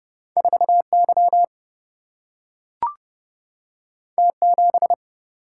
Morse code, heard as a keyed tone, about 30 wpm, 700 Hz. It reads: 4Y T7
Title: Single-Letter Single-Number 30wpm